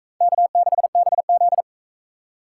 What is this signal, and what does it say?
Morse code, heard as a keyed tone, about 39 wpm, 700 Hz. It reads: K6BZ